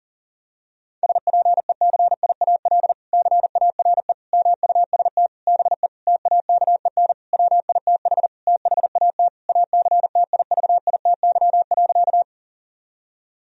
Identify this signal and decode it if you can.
Morse code, heard as a keyed tone, about 40 wpm, 700 Hz. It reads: SPECIAL CARE MUST BE TAKEN WITH THAT ACTIVITY.
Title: Sentences from Top 1000 Words - Review Part 4 - 40wpm